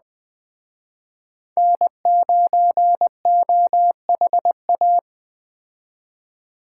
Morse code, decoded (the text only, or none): N9OHA